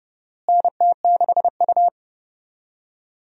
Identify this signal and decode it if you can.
Morse code, heard as a keyed tone, about 30 wpm, 700 Hz. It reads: NT6U
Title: USA Call Signs - 4 Characters - Repeated 2x - 30wpm